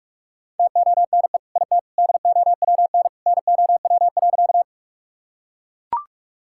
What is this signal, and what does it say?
Morse code, heard as a keyed tone, about 41 wpm, 700 Hz. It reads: TONE IT DOWN NOW.
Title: Sentences from Top 1600 Words 45wpm